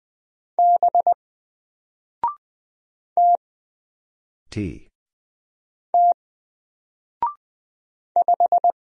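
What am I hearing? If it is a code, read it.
Morse code, heard as a keyed tone, about 20 wpm, 700 Hz. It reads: B T T 5
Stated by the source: Single letter and number Morse drill at 20 wpm, from N4LCD.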